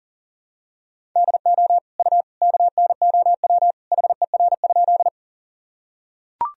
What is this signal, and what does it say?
Morse code, heard as a keyed tone, about 40 wpm, 700 Hz. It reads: DO U KNOW HER?
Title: Sentences from Top 100 Words and Abbreviations 40wpm